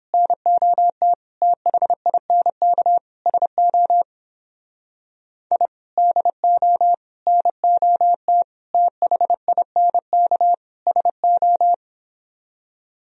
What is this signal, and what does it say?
Morse code, heard as a keyed tone, about 31 wpm, 700 Hz. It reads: NOT THINK SO I DO NOT THINK SO